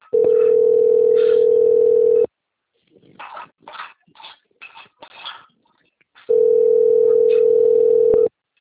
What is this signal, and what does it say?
Ringback tone (North America), 440 + 480 Hz.
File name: out- - - - - .10777